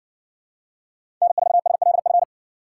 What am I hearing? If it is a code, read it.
Morse code, heard as a keyed tone, about 60 wpm, 700 Hz. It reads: N4SLF